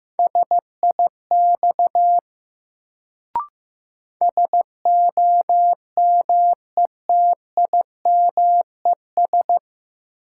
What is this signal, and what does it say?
Morse code, heard as a keyed tone, about 15 wpm, 700 Hz. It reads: SIX SOMETIMES
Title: Sentences from Top 600 Words - New Word Drill - 15wpm